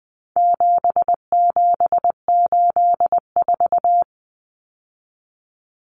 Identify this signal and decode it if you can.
Morse code, heard as a keyed tone, about 20 wpm, 700 Hz. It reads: 7784
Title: Common Cut Numbers - 4-Digit 20wpm